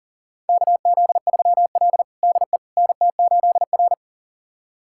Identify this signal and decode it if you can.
Morse code, heard as a keyed tone, about 40 wpm, 700 Hz. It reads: KZ3L DE NT8R
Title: USA Callsign DE Callsign - 4 Characters - 40wpm